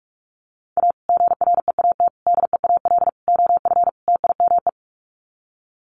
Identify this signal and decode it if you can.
Morse code, heard as a keyed tone, about 44 wpm, 690 Hz. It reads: A GREAT DEAL OF TIME
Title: Sets of 5 Words Encore2 45wpm